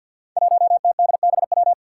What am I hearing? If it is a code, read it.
Morse code, heard as a keyed tone, about 53 wpm, 700 Hz. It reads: 1TDBW
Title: Five Challenging Letter-Number Combinations 50wpm